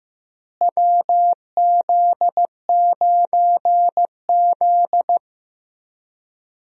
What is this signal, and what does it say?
Morse code, heard as a keyed tone, about 16 wpm, 700 Hz. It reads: WZ9Z
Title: USA Call Signs - 4 Characters - Repeated 3x - 15wpm